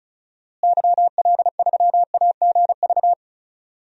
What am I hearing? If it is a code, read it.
Morse code, heard as a keyed tone, about 36 wpm, 700 Hz. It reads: YL3AGV
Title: DXCC-100 Call Signs - 6 Characters - Repeated 2x - 35wpm